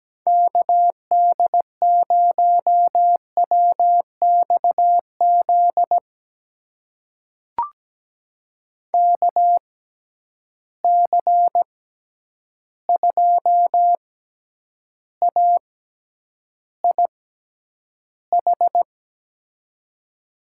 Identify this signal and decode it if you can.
Morse code, heard as a keyed tone, about 17 wpm, spaced as for 9 wpm, 700 Hz. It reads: KD0WXZ KC2AIH
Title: USA Call Signs - 6 Characters - 6x Character Spacing - 17wpm